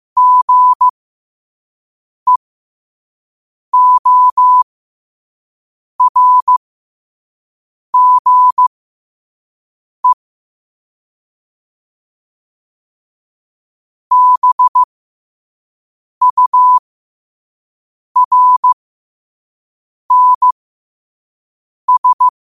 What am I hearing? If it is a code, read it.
Morse code, heard as a keyed tone, about 15 wpm, spaced as for 5 wpm, 1000 Hz. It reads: GEORGE BURNS
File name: qotdfiles / 5wpm